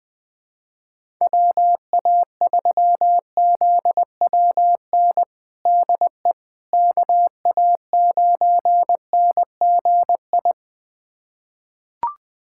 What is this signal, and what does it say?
Morse code, heard as a keyed tone, about 21 wpm, 700 Hz. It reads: WA3ZWN DE KA9NGI
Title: USA Callsign DE Callsign 20wpm